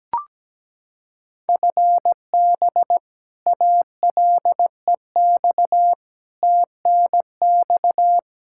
Morse code, heard as a keyed tone, about 17 wpm, 700 Hz. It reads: FB ALEX TNX
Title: Practice QSO - Exchange 3 - More Info - 17wpm